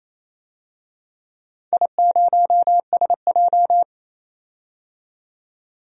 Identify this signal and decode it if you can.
Morse code, heard as a keyed tone, about 29 wpm, 700 Hz. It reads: I0SJ
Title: DXCC-100 Call Signs - 4 Characters - 28wpm